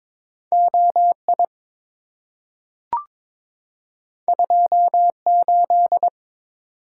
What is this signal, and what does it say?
Morse code, heard as a keyed tone, about 22 wpm, 700 Hz. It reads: OI 28
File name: practice / 22wpm / Two Letter-Number Combinations - 22wpm